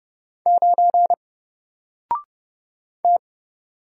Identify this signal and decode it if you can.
Morse code, heard as a keyed tone, about 30 wpm, 700 Hz. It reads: 9 T
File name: Single-Letter-Number-30wpm